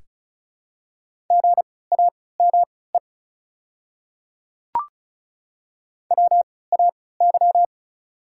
Morse code, heard as a keyed tone, about 36 wpm, spaced as for 20 wpm, 690 Hz. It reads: GAME WAY
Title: 3x Character Spacing - Top 500 Words - 35wpm